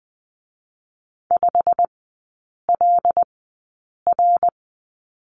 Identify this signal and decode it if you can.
Morse code, heard as a keyed tone, about 20 wpm, spaced as for 8 wpm, 700 Hz. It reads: 5LR
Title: 093 Lesson 16e - Letter c - Three Characters 20wpm